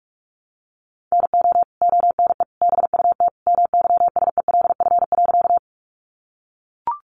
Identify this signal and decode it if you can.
Morse code, heard as a keyed tone, about 44 wpm, 700 Hz. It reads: NO ONE BUT MYSELF.
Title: Sentences from Top 1300 Words 45wpm